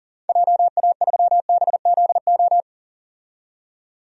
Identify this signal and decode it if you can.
Morse code, heard as a keyed tone, about 40 wpm, 700 Hz. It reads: JA3BZO
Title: DXCC-100 Call Signs - 6 Characters - 40wpm